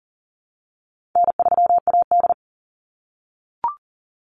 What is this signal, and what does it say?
Morse code, heard as a keyed tone, about 40 wpm, 700 Hz. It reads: N3AD
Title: Easy Call Signs - 40wpm